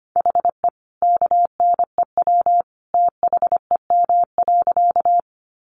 Morse code, heard as a keyed tone, about 25 wpm, 700 Hz. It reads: HE KNEW THEM.
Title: Sentences from Top 400 Words 25wpm